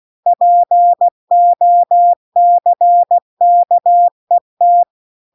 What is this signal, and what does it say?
Morse code, heard as a keyed tone, about 16 wpm, 700 Hz. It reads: POCKET